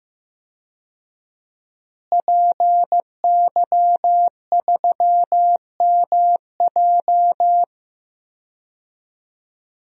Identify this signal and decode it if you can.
Morse code, heard as a keyed tone, about 15 wpm, 700 Hz. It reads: PY3MJ
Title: DXCC-100 Call Signs - 5 Characters - Repeated 2x - 15wpm